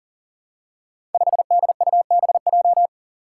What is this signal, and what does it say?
Morse code, heard as a keyed tone, about 41 wpm, 700 Hz. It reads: 5DUBJ
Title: Five Challenging Letter-Number Combinations 40wpm